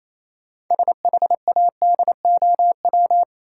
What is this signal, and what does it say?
Morse code, heard as a keyed tone, about 28 wpm, 700 Hz. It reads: SHADOW
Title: Sets of 5 Words Encore - New Word Drill - 28wpm